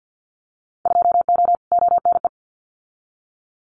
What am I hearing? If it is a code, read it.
Morse code, heard as a keyed tone, about 50 wpm, 700 Hz. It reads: 2O ONE
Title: Winter Field Day Exchanges Encore 50wpm